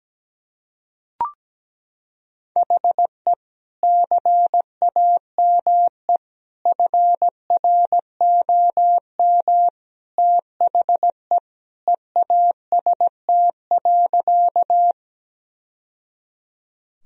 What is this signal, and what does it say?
Morse code, heard as a keyed tone, about 17 wpm, 700 Hz. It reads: HE CAME FROM THE EAST.
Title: Sentences from Top 1000 Words - Review Part 3 - 17wpm